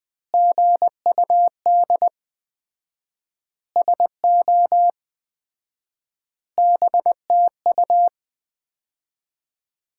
Morse code, heard as a keyed tone, about 20 wpm, 700 Hz. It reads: GUD SO BTU
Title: Common QSO Phrases - 4x Word-Spacing - 20wpm